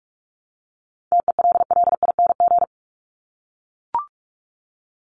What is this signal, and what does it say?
Morse code, heard as a keyed tone, about 44 wpm, 700 Hz. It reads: TELLING
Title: ING Suffix - Top 100 Words with Base Words 45wpm